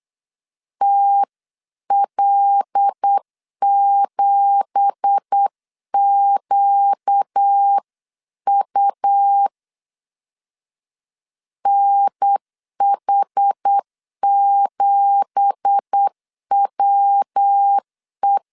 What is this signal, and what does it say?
Morse code, heard as a keyed tone, about 8 wpm, 800 Hz. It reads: TL7QU NH7WE
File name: 8wpm